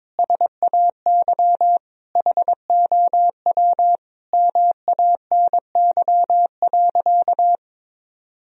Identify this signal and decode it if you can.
Morse code, heard as a keyed tone, about 22 wpm, 700 Hz. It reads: SAY HOW MANY.
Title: Sentences from Top 1000 Words - Review Part 1 - 22wpm